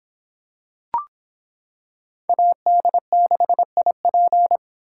Morse code, heard as a keyed tone, about 25 wpm, 700 Hz. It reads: AD6IP